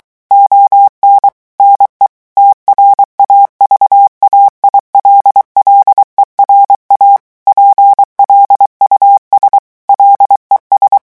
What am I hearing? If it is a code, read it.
Morse code, heard as a keyed tone, about 23 wpm, 800 Hz. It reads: ON NE TRAVAILLERA PLUS LES
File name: morse mp3 / revision9 citations humoristiques vitesse 22 mots minute